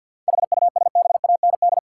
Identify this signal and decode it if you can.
Morse code, heard as a keyed tone, about 50 wpm, 700 Hz. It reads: HUSBAND
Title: Seven Letter Words - 50wpm